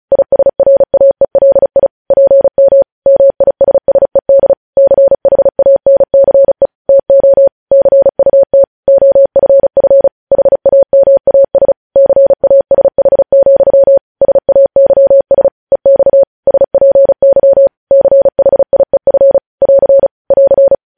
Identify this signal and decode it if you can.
Morse code, heard as a keyed tone, about 36 wpm, 550 Hz. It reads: ISRAELI PM MISSED CHANCE TO CUT OFF HAMAS CASH, SAYS EX SPY CHIEF <AR> <AR>